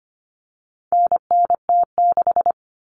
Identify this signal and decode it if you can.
Morse code, heard as a keyed tone, about 25 wpm, 700 Hz. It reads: NNT6
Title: Common Cut Numbers - 4-Digit 25wpm